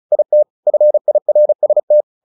Morse code, heard as a keyed tone, about 34 wpm, 600 Hz. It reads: IT FIRST
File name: the old man and the sea 2 35wpm